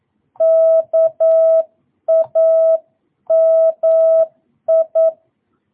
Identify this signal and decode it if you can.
Morse code, heard as a keyed tone, about 9 wpm, 660 Hz. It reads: KAMI